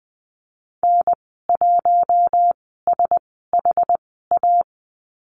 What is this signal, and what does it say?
Morse code, heard as a keyed tone, about 20 wpm, spaced as for 14 wpm, 700 Hz. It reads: N1SHA
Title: 062 Lesson 11g - Letter h - Call Signs 20wpm